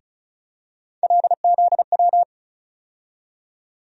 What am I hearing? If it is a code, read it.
Morse code, heard as a keyed tone, about 36 wpm, 700 Hz. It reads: LZW
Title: USA Call Sign Postfixes - 3 Letters 35wpm